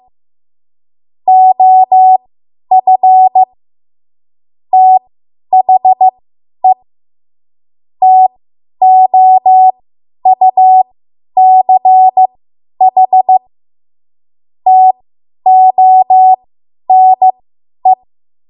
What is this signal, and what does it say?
Morse code, heard as a keyed tone, about 15 wpm, spaced as for 10 wpm, 750 Hz. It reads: OF THE TOUCH TONE
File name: Morse / 10WPM